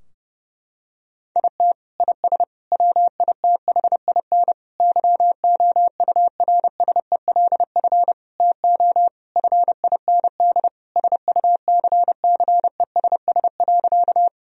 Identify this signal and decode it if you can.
Morse code, heard as a keyed tone, about 30 wpm, 700 Hz. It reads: IT IS WITHIN YOURSELF TO FIND SUCCESS.
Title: Sentences from Top 1000 Words - Review Part 4 - 30wpm